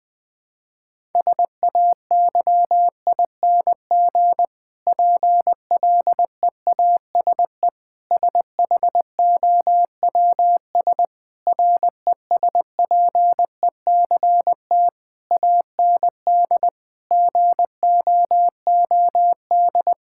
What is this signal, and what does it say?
Morse code, heard as a keyed tone, about 19 wpm, 700 Hz. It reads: SAYING PLEASE SHOWS RESPECT AND GOOD
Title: Difficult Sentences with Hints - 20wpm